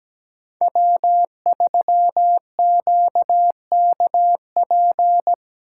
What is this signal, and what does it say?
Morse code, heard as a keyed tone, about 17 wpm, 700 Hz. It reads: W3QKP
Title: USA Call Signs - 5 Characters - 4x Character Spacing - 17wpm